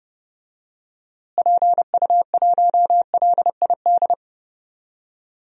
Morse code, heard as a keyed tone, about 31 wpm, 700 Hz. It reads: PU1LID
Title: DXCC-100 Call Signs - 6 Characters - Repeated 3x - 30wpm